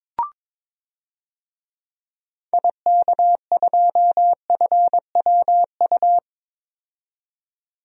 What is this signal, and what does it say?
Morse code, heard as a keyed tone, about 21 wpm, 700 Hz. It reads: IK2FWU
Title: DXCC-100 Call Signs - 6 Characters - 22wpm